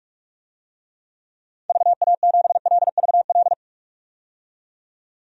Morse code, heard as a keyed tone, about 43 wpm, 700 Hz. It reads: VA7LVL
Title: DXCC-100 Call Signs - 6 Characters - 45wpm